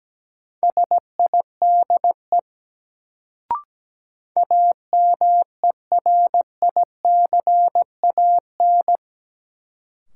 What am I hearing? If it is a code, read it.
Morse code, heard as a keyed tone, about 17 wpm, 700 Hz. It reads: SIDE AMERICAN
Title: Sets of 2 Words - New Word Drill - 17wpm